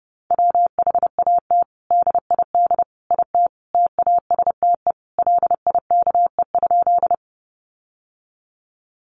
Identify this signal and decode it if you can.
Morse code, heard as a keyed tone, about 30 wpm, 700 Hz. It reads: WHAT DID IT TASTE LIKE?
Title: Sentences from Top 1100 Words 30wpm